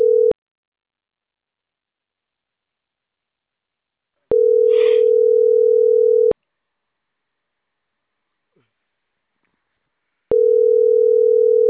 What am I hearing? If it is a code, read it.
Ringback tone (North America), 440 + 480 Hz.